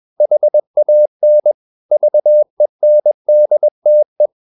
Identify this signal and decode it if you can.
Morse code, heard as a keyed tone, about 21 wpm, 600 Hz. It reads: HAN VENDTE
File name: Ferie QRS 21wpm